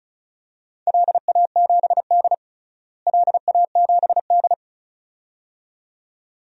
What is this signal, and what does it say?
Morse code, heard as a keyed tone, about 36 wpm, 700 Hz. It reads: LA7D LA7D
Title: DXCC-100 Call Signs - 4 Characters - Repeated 2x - 35wpm